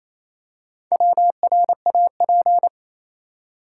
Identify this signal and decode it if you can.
Morse code, heard as a keyed tone, about 28 wpm, 700 Hz. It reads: WRAP
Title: Four Letter Words - 28wpm